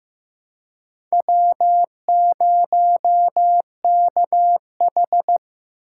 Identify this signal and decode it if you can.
Morse code, heard as a keyed tone, about 15 wpm, 700 Hz. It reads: W0KH
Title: USA Call Signs - 4 Characters - 2x Character Spacing - 15wpm